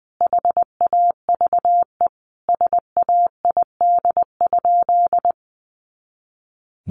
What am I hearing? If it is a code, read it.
Morse code, heard as a keyed tone, about 20 wpm, 700 Hz. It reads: HAVE SAID?
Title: Sentences from Top 100 Words 20wpm